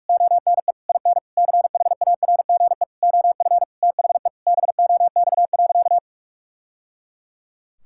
Morse code, heard as a keyed tone, about 46 wpm, 700 Hz. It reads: ONE IN CHARGE OF THE BOX.
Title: Sentences from Top 1000 Words - Review Part 6 - 45wpm